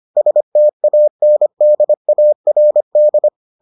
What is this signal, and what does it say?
Morse code, heard as a keyed tone, about 25 wpm, 600 Hz. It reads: STANDARD